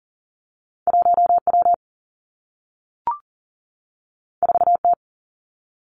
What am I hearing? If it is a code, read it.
Morse code, heard as a keyed tone, about 42 wpm, 700 Hz. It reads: 1 W 4 T
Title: Two Challenging Letter-Number Combinations 40wpm